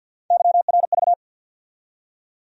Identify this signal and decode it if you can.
Morse code, heard as a keyed tone, about 50 wpm, 700 Hz. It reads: YRV
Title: USA Call Sign Postfixes - 3 Letters 50wpm